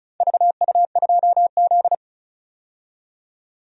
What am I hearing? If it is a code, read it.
Morse code, heard as a keyed tone, about 34 wpm, 700 Hz. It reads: VU2Z
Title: DXCC-100 Call Signs - 4 Characters - 35wpm